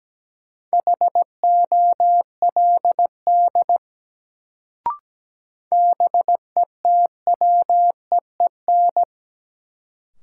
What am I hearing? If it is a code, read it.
Morse code, heard as a keyed tone, about 17 wpm, 700 Hz. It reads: HOLD BETWEEN
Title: Sentences from Top 400 Words - New Word Drill - 17wpm